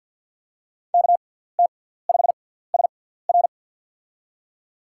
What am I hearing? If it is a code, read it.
Morse code, heard as a keyed tone, about 48 wpm, spaced as for 17 wpm, 700 Hz. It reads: KT5SR
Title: USA Call Signs - 5 Characters - 6x Character Spacing - 50wpm